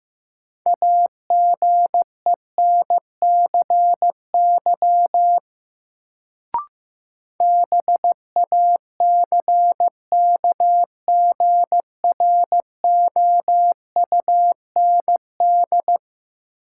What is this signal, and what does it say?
Morse code, heard as a keyed tone, about 15 wpm, 700 Hz. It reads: AGENCY BACKGROUND
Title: Sentences from Top 1300 Words - New Word Drill - 15wpm